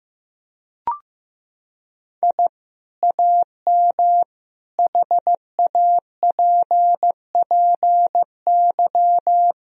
Morse code, heard as a keyed tone, about 15 wpm, 700 Hz. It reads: I AM HAPPY
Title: Sentences from Top 1000 Words - Review Part 2 - 15wpm